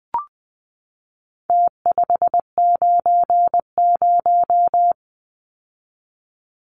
Morse code, heard as a keyed tone, about 20 wpm, 700 Hz. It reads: T590
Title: Numbers - Contest Style - 4-Digit 20wpm